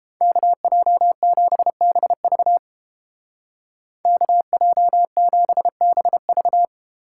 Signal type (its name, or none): Morse code